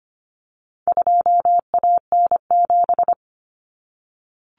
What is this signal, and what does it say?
Morse code, heard as a keyed tone, about 25 wpm, 700 Hz. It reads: 2AN7
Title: Common Cut Numbers - 4-Digit 25wpm